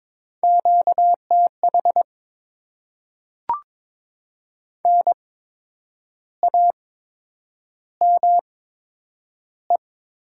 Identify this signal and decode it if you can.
Morse code, heard as a keyed tone, about 22 wpm, spaced as for 8 wpm, 700 Hz. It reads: QTH NAME
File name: Top 20 QSO Elements - 8x Character Spacing - 22wpm